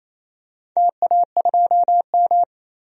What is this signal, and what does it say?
Morse code, heard as a keyed tone, about 28 wpm, 700 Hz. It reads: TA2M
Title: DXCC-100 Call Signs - 4 Characters - 28wpm